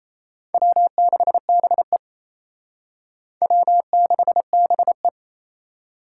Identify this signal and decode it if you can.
Morse code, heard as a keyed tone, about 35 wpm, 700 Hz. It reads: W6BE W6BE
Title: Speed Racing - USA 1x2 Call Signs 28wpm